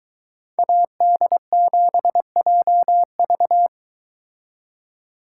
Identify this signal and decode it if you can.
Morse code, heard as a keyed tone, about 23 wpm, 700 Hz. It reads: AD7JV